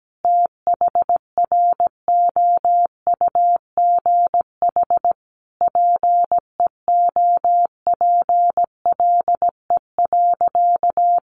Morse code, heard as a keyed tone, about 17 wpm, 700 Hz. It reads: THROUGH PEOPLE.
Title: Sentences from Top 200 Words 17wpm